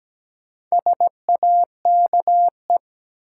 Morse code, heard as a keyed tone, about 17 wpm, 700 Hz. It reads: SAKE
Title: Sets of 4 Words - New Word Drill - 17wpm